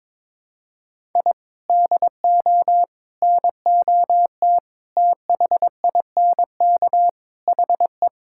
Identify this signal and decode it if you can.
Morse code, heard as a keyed tone, about 22 wpm, 700 Hz. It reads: I DO NOT THINK HE
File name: practice / 22wpm / Build A Phrase - 5-Word Phrases - 22wpm